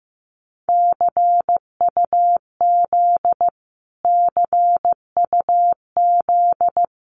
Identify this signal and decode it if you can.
Morse code, heard as a keyed tone, about 15 wpm, 700 Hz. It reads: CUZ CUZ
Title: Abbreviations Repeated 3x - 15wpm